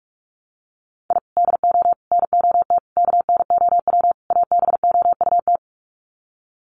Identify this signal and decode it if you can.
Morse code, heard as a keyed tone, about 44 wpm, 700 Hz. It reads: I DO NOT KNOW ABOUT
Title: Sets of 5 Words 45wpm